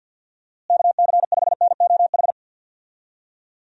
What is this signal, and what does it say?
Morse code, heard as a keyed tone, about 51 wpm, 700 Hz. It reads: KC5NOH
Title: USA Call Signs - 6 Characters - Repeated 3x - 50wpm